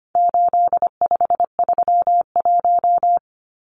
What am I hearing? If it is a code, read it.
Morse code, heard as a keyed tone, about 25 wpm, 700 Hz. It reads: 8531
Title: Numbers - 4-Digit 25wpm